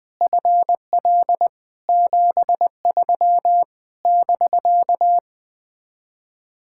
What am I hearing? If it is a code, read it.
Morse code, heard as a keyed tone, about 20 wpm, 700 Hz. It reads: FL 73 <BK>
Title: Speed-Racing Common QSO Phrases Encore - 17wpm